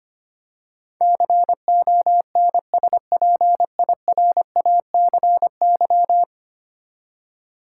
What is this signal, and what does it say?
Morse code, heard as a keyed tone, about 25 wpm, 700 Hz. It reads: CONSPIRACY
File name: Ten Letter Words - 25wpm